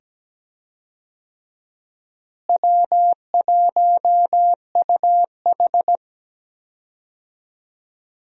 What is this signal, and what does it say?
Morse code, heard as a keyed tone, about 17 wpm, 700 Hz. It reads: W1UH
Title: USA Call Signs - 4 Characters - Repeated 2x - 17wpm